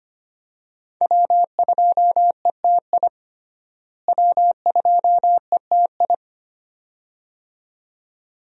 Morse code, heard as a keyed tone, about 25 wpm, 700 Hz. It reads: W2ETI W2ETI
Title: USA Call Signs - 5 Characters - Repeated 2x - 25wpm